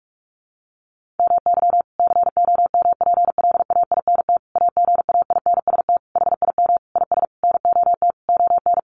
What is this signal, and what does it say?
Morse code, heard as a keyed tone, about 44 wpm, 700 Hz. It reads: MY COMPLAINT AGAINST HIM IS NOT ON
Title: Sentences from Top 1400 Words 45wpm